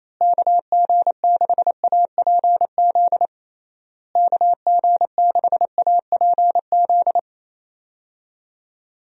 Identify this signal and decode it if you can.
Morse code, heard as a keyed tone, about 28 wpm, 700 Hz. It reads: KG6APZ KG6APZ
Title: USA Call Signs - 6 Characters - Repeated 3x - 28wpm